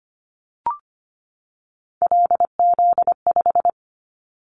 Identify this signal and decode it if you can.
Morse code, heard as a keyed tone, about 25 wpm, 700 Hz. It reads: LZ5
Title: Callsign Prefixes for Top 25 DXCC - 25wpm